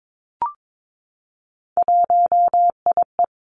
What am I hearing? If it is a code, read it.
Morse code, heard as a keyed tone, about 22 wpm, 700 Hz. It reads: 1IE
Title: Three Challenging Letter-Number Combinations 22wpm